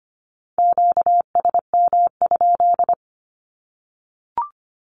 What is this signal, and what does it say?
Morse code, heard as a keyed tone, about 25 wpm, 700 Hz. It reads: QSM?
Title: Q-Codes 25wpm